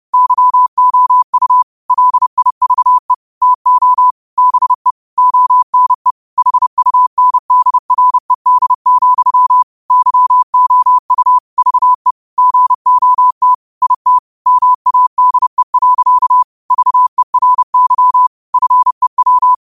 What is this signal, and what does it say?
Morse code, heard as a keyed tone, about 30 wpm, 1000 Hz. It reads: YOU LIVE TO BE ONE HUNDRED, YOU VE GOT IT MADE. VERY FEW
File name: qotdfiles / 30wpm